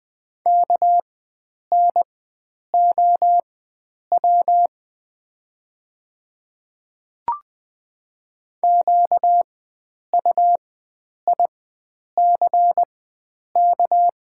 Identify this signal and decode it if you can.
Morse code, heard as a keyed tone, about 20 wpm, spaced as for 9 wpm, 690 Hz. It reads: KNOW QUICK
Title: 4x Character Spacing - Top 500 Words - 20wpm